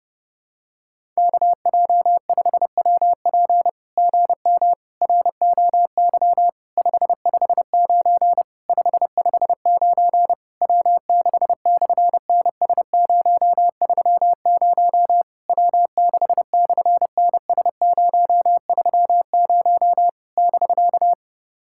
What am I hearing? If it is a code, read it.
Morse code, heard as a keyed tone, about 28 wpm, 700 Hz. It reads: KJ5WP GM ROY 559 559 W6/NS030 W6/NS030 <BK>
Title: SOTA QSO Practice 30wpm